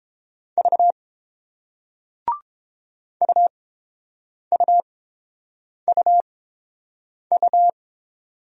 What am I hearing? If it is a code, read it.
Morse code, heard as a keyed tone, about 30 wpm, spaced as for 7 wpm, 700 Hz. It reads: V UUUU